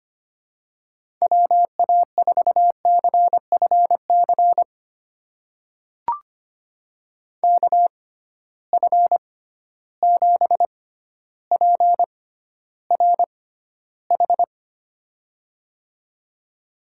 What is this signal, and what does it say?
Morse code, heard as a keyed tone, about 25 wpm, spaced as for 13 wpm, 700 Hz. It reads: WA4CFC KF7PRH
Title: USA Call Signs - 6 Characters - 6x Character Spacing - 25wpm